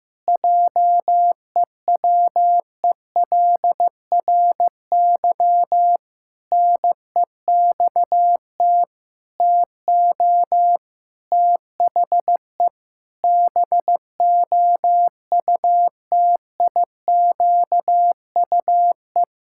Morse code, heard as a keyed tone, about 15 wpm, 700 Hz. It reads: JEWELRY NEXT TO THE BOUTIQUE
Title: Pangram Challenge v2 - 15wpm